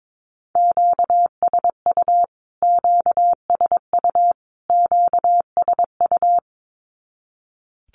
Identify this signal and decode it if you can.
Morse code, heard as a keyed tone, about 22 wpm, 700 Hz. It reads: QSU QSU QSU
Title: Q-Codes Repeated 3x 22wpm